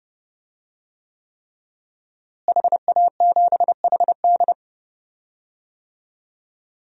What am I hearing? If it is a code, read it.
Morse code, heard as a keyed tone, about 32 wpm, 700 Hz. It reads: HA7HD
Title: DXCC-100 Call Signs - 5 Characters - Repeated 3x - 30wpm